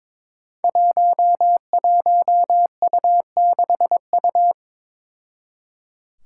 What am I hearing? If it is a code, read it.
Morse code, heard as a keyed tone, about 22 wpm, 700 Hz. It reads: 11U6U
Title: Five Challenging Letter-Number Combinations 22wpm